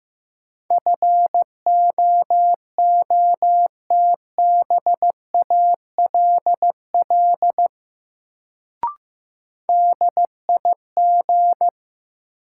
Morse code, heard as a keyed tone, about 15 wpm, 700 Hz. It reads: FOOTBALL DIG
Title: Sentences from Top 1600 Words - New Word Drill Alt - 15wpm